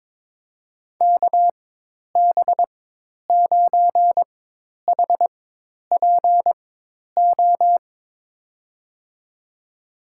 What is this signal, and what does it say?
Morse code, heard as a keyed tone, about 22 wpm, spaced as for 10 wpm, 700 Hz. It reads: KB9HPO
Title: USA Call Signs - 6 Characters - 4x Character Spacing - 22wpm